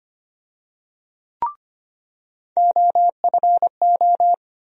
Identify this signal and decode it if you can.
Morse code, heard as a keyed tone, about 25 wpm, 700 Hz. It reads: OFO
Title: USA Call Sign Postfixes - 3 Letters 25wpm